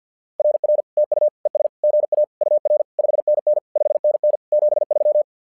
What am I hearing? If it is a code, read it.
Morse code, heard as a keyed tone, about 48 wpm, 600 Hz. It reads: RR TU ES GA UR 5NN 5NN 73
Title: SOTA QSO Practice 50wpm